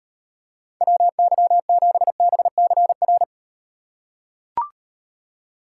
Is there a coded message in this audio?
Yes: Morse code from 0.8 to 3.2 s, heard as a keyed tone, about 34 wpm, 700 Hz. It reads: WY7BCR